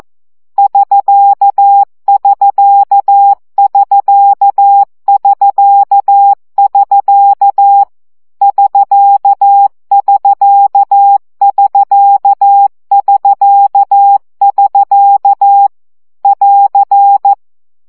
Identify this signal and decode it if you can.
Morse code, heard as a keyed tone, about 14 wpm, 800 Hz. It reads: <SK><SK><SK><SK><SK> <SK><SK><SK><SK><SK> <AR>
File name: CW gyakorlas sk 12 baud